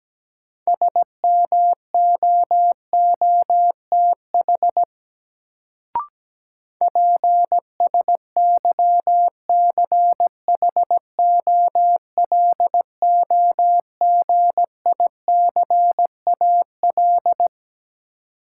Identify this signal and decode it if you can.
Morse code, heard as a keyed tone, about 17 wpm, 700 Hz. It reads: SMOOTH PSYCHOLOGICAL
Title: Sentences from Top 1400 Words - New Word Drill - 17wpm